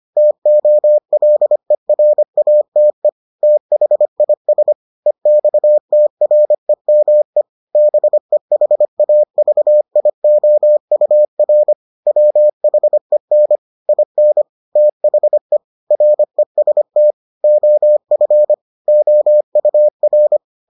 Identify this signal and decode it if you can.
Morse code, heard as a keyed tone, about 25 wpm, 600 Hz. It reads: TOLERATE THIS EXTREME BEHAVIOUR WHEN IN THE REST OF OUR